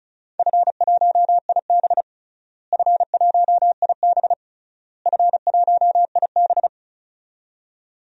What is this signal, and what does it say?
Morse code, heard as a keyed tone, about 34 wpm, 700 Hz. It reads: F1IB F1IB F1IB
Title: DXCC-100 Call Signs - 4 Characters - Repeated 3x - 35wpm